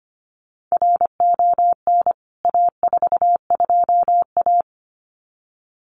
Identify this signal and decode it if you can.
Morse code, heard as a keyed tone, about 25 wpm, 700 Hz. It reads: RON A42A
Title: CWT Exchange 25wpm